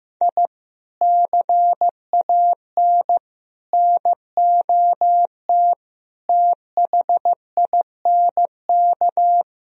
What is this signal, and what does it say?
Morse code, heard as a keyed tone, about 15 wpm, 700 Hz. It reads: I CAN NOT THINK
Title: Speed-Racing - Sets of 5 Words 15wpm